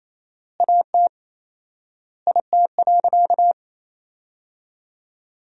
Morse code, heard as a keyed tone, about 28 wpm, 700 Hz. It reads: AT IT.
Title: 4x Word-Spacing - Sentences from Top 300 Words - 28wpm